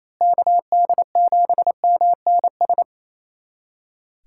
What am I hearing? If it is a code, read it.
Morse code, heard as a keyed tone, about 28 wpm, 700 Hz. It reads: KD7MNS
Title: USA Callsign DE Callsign 28wpm